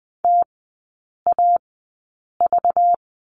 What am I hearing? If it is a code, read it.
Morse code, heard as a keyed tone, about 20 wpm, spaced as for 8 wpm, 700 Hz. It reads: TAV
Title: 191 Lesson 31e - Letter v - Three Characters 20wpm